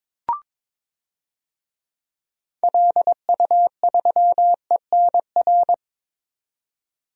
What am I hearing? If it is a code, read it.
Morse code, heard as a keyed tone, about 22 wpm, 700 Hz. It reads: LU3ENR
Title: DXCC-100 Call Signs - 6 Characters - 22wpm